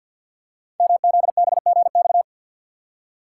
Morse code, heard as a keyed tone, about 52 wpm, 700 Hz. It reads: MZBGX
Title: Code Groups - 5 Letters - 50wpm